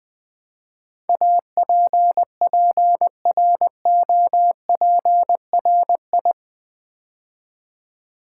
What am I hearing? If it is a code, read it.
Morse code, heard as a keyed tone, about 20 wpm, 700 Hz. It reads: APPROPRI
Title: Build A Word - 4 Letters At A Time - 11-Letter Words - 20wpm